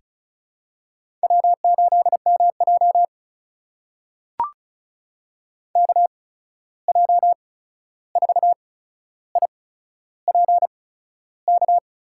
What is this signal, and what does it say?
Morse code, heard as a keyed tone, about 36 wpm, spaced as for 13 wpm, 700 Hz. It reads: W8MJ KJ4IPK
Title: USA Call Signs - 8x Character Spacing - 35wpm